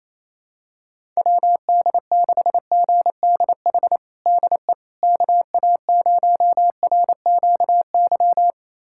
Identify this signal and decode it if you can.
Morse code, heard as a keyed tone, about 28 wpm, 700 Hz. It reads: WD6GDH DE KA0RQY
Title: USA Callsign DE Callsign - 6 Characters - 28wpm